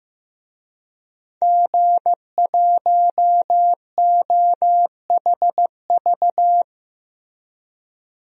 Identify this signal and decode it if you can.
Morse code, heard as a keyed tone, about 15 wpm, 700 Hz. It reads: G1OHV
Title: DXCC-100 Call Signs - 5 Characters - Repeated 2x - 15wpm